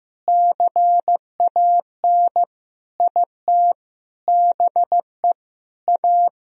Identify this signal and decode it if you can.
Morse code, heard as a keyed tone, about 16 wpm, 700 Hz. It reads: CAN IT BE A
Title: Sentences from Top 1000 Words - Review Part 2 - 15wpm